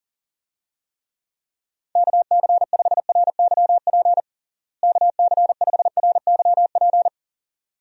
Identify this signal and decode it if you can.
Morse code, heard as a keyed tone, about 40 wpm, 700 Hz. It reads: KC5RYP KC5RYP
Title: USA Call Signs - 6 Characters - Repeated 2x - 40wpm